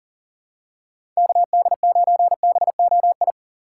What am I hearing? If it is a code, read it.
Morse code, heard as a keyed tone, about 40 wpm, 700 Hz. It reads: KD9BOI
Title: USA Call Signs - 6x Character Spacing - 40wpm